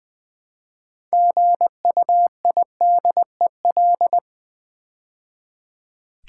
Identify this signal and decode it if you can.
Morse code, heard as a keyed tone, about 20 wpm, 700 Hz. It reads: GUIDEL
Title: Build A Word - 3 Letters At A Time - 9-Letter Words - 20wpm